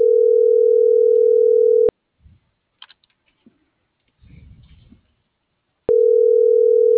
Ringback tone (North America), 440 + 480 Hz.